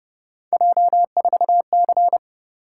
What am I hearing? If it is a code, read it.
Morse code, heard as a keyed tone, about 29 wpm, 700 Hz. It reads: J4C